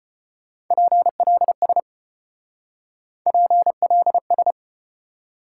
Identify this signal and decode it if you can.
Morse code, heard as a keyed tone, about 33 wpm, 700 Hz. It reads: PLS PLS